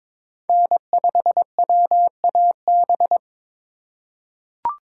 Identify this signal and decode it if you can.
Morse code, heard as a keyed tone, about 22 wpm, 700 Hz. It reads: N5WAB
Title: USA Call Signs - 3x Character Spacing - 22wpm